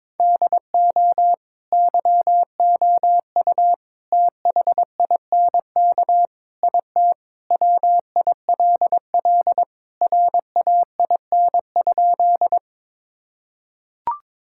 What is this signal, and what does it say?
Morse code, heard as a keyed tone, about 22 wpm, 700 Hz. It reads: DO YOU THINK IT WILL RAIN?
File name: practice / 22wpm / Sentences with 4-Word Phrases - 22wpm